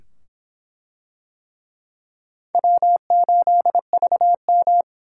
Morse code, heard as a keyed tone, about 23 wpm, 700 Hz. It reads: W8VM